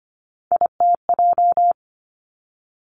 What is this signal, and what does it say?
Morse code, heard as a keyed tone, about 25 wpm, 700 Hz. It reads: ITJ
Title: Three Challenging Letter-Number Combinations 25wpm